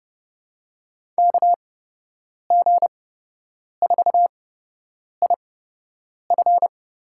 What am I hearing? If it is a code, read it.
Morse code, heard as a keyed tone, about 30 wpm, spaced as for 8 wpm, 700 Hz. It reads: KG4IF
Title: USA Call Signs - 8x Character Spacing - 30wpm